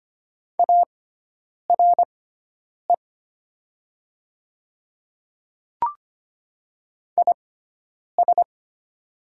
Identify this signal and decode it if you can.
Morse code, heard as a keyed tone, about 25 wpm, spaced as for 9 wpm, 700 Hz. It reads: ARE IS
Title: 6x Character Spacing - Top 100 Words - 25wpm